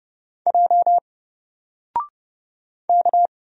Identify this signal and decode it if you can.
Morse code, heard as a keyed tone, about 29 wpm, 700 Hz. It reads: J K
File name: Single-Letter-Number-30wpm